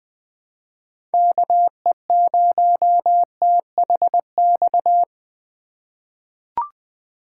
Morse code, heard as a keyed tone, about 20 wpm, 700 Hz. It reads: KE0THX
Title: USA Call Signs - 6 Characters - 2x Character Spacing - 20wpm